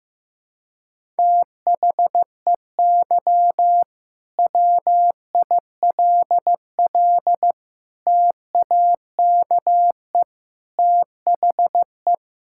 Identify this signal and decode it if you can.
Morse code, heard as a keyed tone, about 16 wpm, 700 Hz. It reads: THEY WILL TAKE THE
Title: Sentences from Top 1000 Words - Review Part 5 - 15wpm